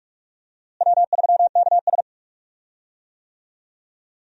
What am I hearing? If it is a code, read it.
Morse code, heard as a keyed tone, about 42 wpm, 700 Hz. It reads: W3KS
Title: USA Call Signs - 4 Characters - Repeated 3x - 45wpm